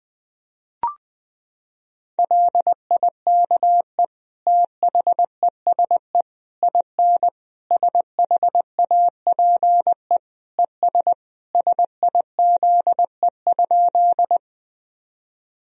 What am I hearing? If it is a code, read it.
Morse code, heard as a keyed tone, about 20 wpm, 700 Hz. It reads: LIKE THESE IN SHAPE ES SIZE?
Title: Sentences from Top 500 Words and Abbreviations 20wpm